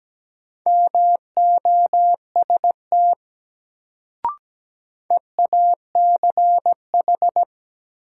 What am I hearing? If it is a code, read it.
Morse code, heard as a keyed tone, about 17 wpm, 700 Hz. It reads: MOST EACH
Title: Sets of 2 Words - New Word Drill - 17wpm